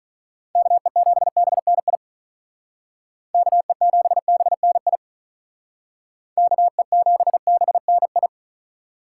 Morse code, heard as a keyed tone, about 43 wpm, 700 Hz. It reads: KE7BNI KE7BNI KE7BNI
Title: Speed Racing - USA 2x3 Call Signs 35wpm